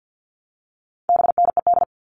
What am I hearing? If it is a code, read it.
Morse code, heard as a keyed tone, about 50 wpm, 700 Hz. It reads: BNED
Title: All Cut Numbers - 4-Digit 50wpm